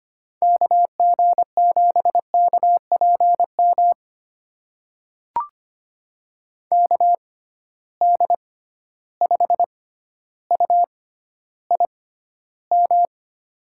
Morse code, heard as a keyed tone, about 25 wpm, spaced as for 13 wpm, 700 Hz. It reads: KG7KPM KD5UIM
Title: USA Call Signs - 6 Characters - 6x Character Spacing - 25wpm